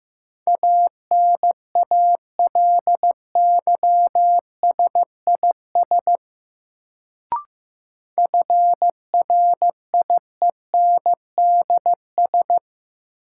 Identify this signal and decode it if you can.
Morse code, heard as a keyed tone, about 15 wpm, 700 Hz. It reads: ANALYSIS FRIENDS
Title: Sentences from Top 1000 Words - Review Part 5 - New Word Drill - 15wpm